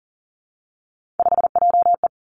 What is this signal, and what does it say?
Morse code, heard as a keyed tone, about 38 wpm, 690 Hz. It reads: 5JE
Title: Three Challenging Letter-Number Combinations 40wpm